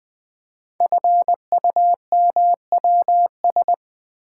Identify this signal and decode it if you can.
Morse code, heard as a keyed tone, about 20 wpm, 700 Hz. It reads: FUMWS
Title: Code Groups - 5 Letters - 20wpm